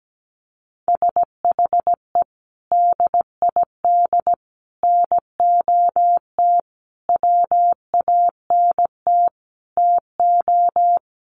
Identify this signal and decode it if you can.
Morse code, heard as a keyed tone, about 17 wpm, 700 Hz. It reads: SHE DID NOT WANT TO
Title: Sets of 5 Words 17wpm